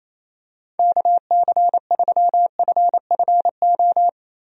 Morse code, heard as a keyed tone, about 28 wpm, 700 Hz. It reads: KC3FFO
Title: USA Call Signs - 6 Characters - Repeated 3x - 28wpm